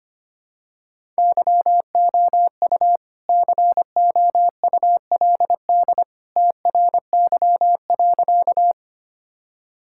Morse code, heard as a keyed tone, about 25 wpm, 700 Hz. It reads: YOU COULD TRY.
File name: practice / 25wpm / Sentences with 2-Word Phrases - 25wpm